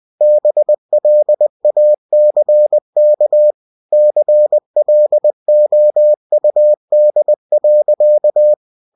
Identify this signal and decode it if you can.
Morse code, heard as a keyed tone, about 20 wpm, 600 Hz. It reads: BLACK CLOUD.
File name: cw-books / winnie-the-pooh / winnie-20wpm-page-0006